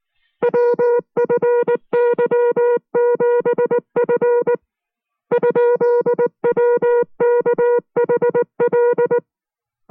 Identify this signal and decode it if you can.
Morse code, heard as a keyed tone, about 19 wpm, 480 Hz. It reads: WFY7F ?WKHL